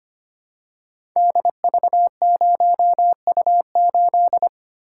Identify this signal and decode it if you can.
Morse code, heard as a keyed tone, about 25 wpm, 700 Hz. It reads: DV0U8
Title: Code Groups - 5 Letter:Numbers - 25wpm